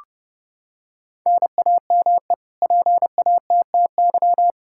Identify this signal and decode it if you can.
Morse code, heard as a keyed tone, about 30 wpm, 700 Hz. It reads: NAME PATTY
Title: K1USN SST QSO Practice 30wpm